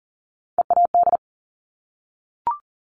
Morse code, heard as a keyed tone, about 41 wpm, 690 Hz. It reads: EAD
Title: Top Three Letter Combinations in Words 40wpm